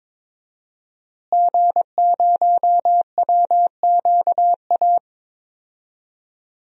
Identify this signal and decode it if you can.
Morse code, heard as a keyed tone, about 22 wpm, 700 Hz. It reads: G0WQA
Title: DXCC-100 Call Signs - 5 Characters - Repeated 2x - 22wpm